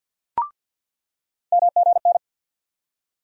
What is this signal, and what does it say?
Morse code, heard as a keyed tone, about 49 wpm, 700 Hz. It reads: MGN
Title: USA Call Sign Postfixes - 3 Letters 50wpm